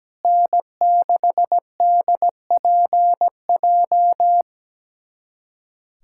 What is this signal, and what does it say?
Morse code, heard as a keyed tone, about 17 wpm, 700 Hz. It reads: N6DPJ
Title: USA Callsign DE Callsign - 5 Characters - 17wpm